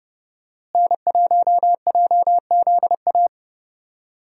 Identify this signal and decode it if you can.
Morse code, heard as a keyed tone, about 30 wpm, 700 Hz. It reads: N1JZA
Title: USA Call Signs - 5 Characters - Repeated 2x - 30wpm